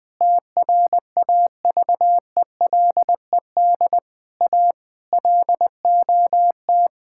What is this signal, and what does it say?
Morse code, heard as a keyed tone, about 20 wpm, 700 Hz. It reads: TRAVELED A LOT
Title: Sentences with 6-Word Phrases - Spoken Beginning - 20wpm